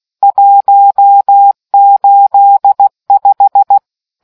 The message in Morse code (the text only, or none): 185